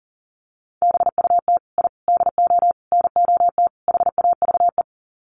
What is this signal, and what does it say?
Morse code, heard as a keyed tone, about 40 wpm, 700 Hz. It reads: BUT I DO NOT HAVE
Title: Sets of 5 Words Encore 40wpm